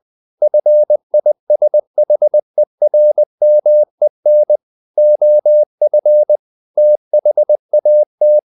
Morse code, heard as a keyed tone, about 20 wpm, 600 Hz. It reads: FISHERMEN OF THAT
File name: the old man and the sea / the old man and the sea part 1 20wpm